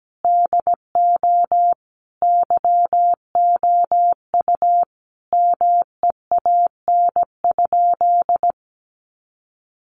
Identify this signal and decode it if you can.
Morse code, heard as a keyed tone, about 17 wpm, 700 Hz. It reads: DO YOU MEAN?
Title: Sentences from Top 1300 Words 17wpm